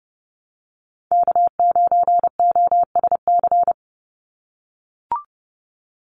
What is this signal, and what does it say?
Morse code, heard as a keyed tone, about 30 wpm, 700 Hz. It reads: K9OSC
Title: NA SOTA Chaser Callsigns 30wpm